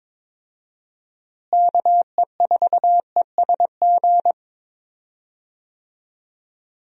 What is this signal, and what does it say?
Morse code, heard as a keyed tone, about 22 wpm, 700 Hz. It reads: KE4ESG
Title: USA Call Signs - 6 Characters - Repeated 3x - 22wpm